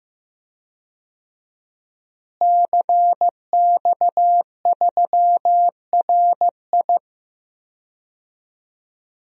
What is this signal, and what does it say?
Morse code, heard as a keyed tone, about 15 wpm, 700 Hz. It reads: CX3RI
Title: DXCC-100 Call Signs - 5 Characters - Repeated 3x - 15wpm